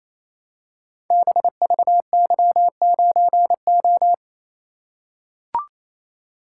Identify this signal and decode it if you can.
Morse code, heard as a keyed tone, about 27 wpm, 700 Hz. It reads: BVY9O
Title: Code Groups - 5 Letter:Numbers - 28wpm